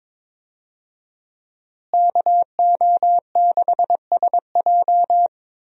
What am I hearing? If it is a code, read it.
Morse code, heard as a keyed tone, about 22 wpm, 700 Hz. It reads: KO6SJ
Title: USA Call Signs - 5 Characters - Repeated 2x - 22wpm